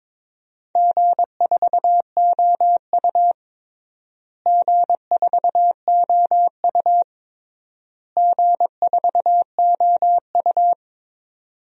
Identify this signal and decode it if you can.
Morse code, heard as a keyed tone, about 21 wpm, 700 Hz. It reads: G4OU G4OU G4OU
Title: DXCC-100 Call Signs - 4 Characters - Repeated 3x - 22wpm